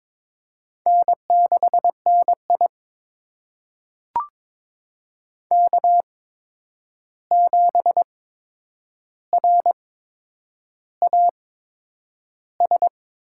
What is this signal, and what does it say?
Morse code, heard as a keyed tone, about 22 wpm, spaced as for 9 wpm, 700 Hz. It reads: N6NI K7RAS
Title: USA Call Signs - 8x Character Spacing - 22wpm